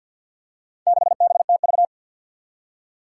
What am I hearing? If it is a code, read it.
Morse code, heard as a keyed tone, about 53 wpm, 700 Hz. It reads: 6BTV
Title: Four Challenging Letter-Number Combinations 50wpm